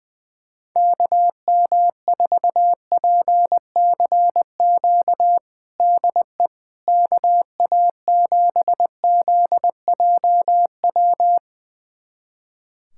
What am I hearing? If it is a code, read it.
Morse code, heard as a keyed tone, about 20 wpm, 700 Hz. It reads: KM4PCQ DE KA7ZJW
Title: USA Callsign DE Callsign - 6 Characters - 20wpm